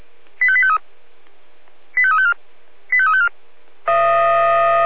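Five-tone selective call (ZVEI), heard as quick runs of five tones, tones of about 70 ms. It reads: 85853 85354 85354
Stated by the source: Five-tone ZVEI selective call received by SDR.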